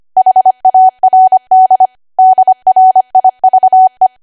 Morse code, heard as a keyed tone, about 25 wpm, 750 Hz. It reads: HARD DRIVE
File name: cwa / W1AW-Level3 / 25WPM